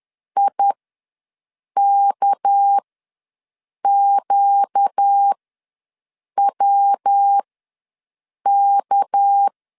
Morse code, heard as a keyed tone, about 10 wpm, spaced as for 6 wpm, 800 Hz. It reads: IKQWK
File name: LEZIONI / 7 GF9Z / 7 GF9 ws 10wpm 4034